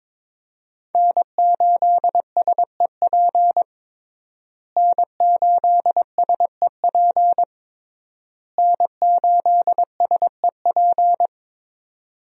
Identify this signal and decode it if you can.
Morse code, heard as a keyed tone, about 22 wpm, 700 Hz. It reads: N8SEP N8SEP N8SEP
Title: USA Call Signs - 5 Characters - Repeated 3x - 22wpm